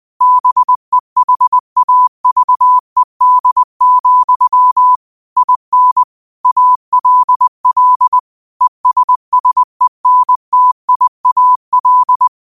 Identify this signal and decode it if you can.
Morse code, heard as a keyed tone, about 20 wpm, 1000 Hz. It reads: BEHAVED, IN ALL ESSENTIAL
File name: qotdfiles / 20wpm